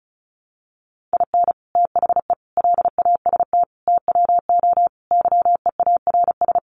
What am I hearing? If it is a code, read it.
Morse code, heard as a keyed tone, about 34 wpm, 700 Hz. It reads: IN THE LAST TWO YEARS
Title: Sets of 5 Words 35wpm